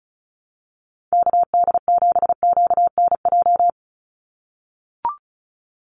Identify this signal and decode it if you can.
Morse code, heard as a keyed tone, about 34 wpm, 700 Hz. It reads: KD7QNJ
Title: USA Call Signs 35wpm